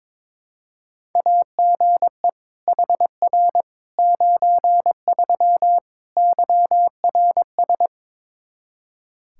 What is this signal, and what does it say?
Morse code, heard as a keyed tone, about 22 wpm, 700 Hz. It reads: AGE HR 93 YRS
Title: Practice QSO - Exchange 3 - More Info - 22wpm